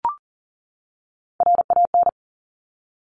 Morse code, heard as a keyed tone, about 41 wpm, 690 Hz. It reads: RAN
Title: Top Three Letter Combinations in Words 40wpm